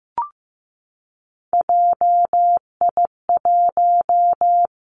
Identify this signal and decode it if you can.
Morse code, heard as a keyed tone, about 15 wpm, 700 Hz. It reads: JI1
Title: Three Challenging Letter-Number Combinations 15wpm